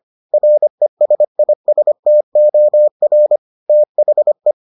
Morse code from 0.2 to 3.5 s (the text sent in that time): RESISTOR